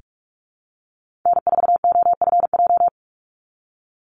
Morse code, heard as a keyed tone, about 44 wpm, 700 Hz. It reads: N4OFJ
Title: USA Call Signs 45wpm